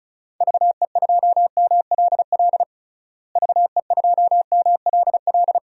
Morse code, heard as a keyed tone, about 36 wpm, 700 Hz. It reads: VE2MLL VE2MLL
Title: DXCC-100 Call Signs - 6 Characters - Repeated 2x - 35wpm